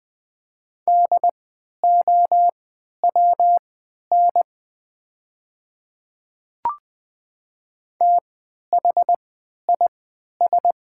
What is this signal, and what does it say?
Morse code, heard as a keyed tone, about 20 wpm, spaced as for 11 wpm, 700 Hz. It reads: DOWN THIS
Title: 3x Character Spacing - Top 200 Words - 20wpm